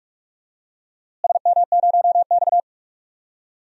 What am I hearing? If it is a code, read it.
Morse code, heard as a keyed tone, about 42 wpm, 700 Hz. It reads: SM0X
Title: DXCC-100 Call Signs - 4 Characters - Repeated 3x - 45wpm